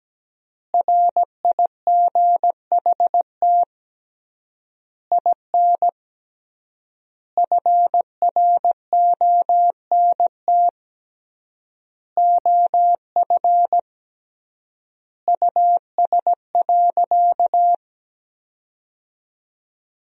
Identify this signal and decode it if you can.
Morse code, heard as a keyed tone, about 17 wpm, 700 Hz. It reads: RIGHT IN FRONT OF US.
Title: 3x Word-Spacing - Sentences from Top 500 Words - 17wpm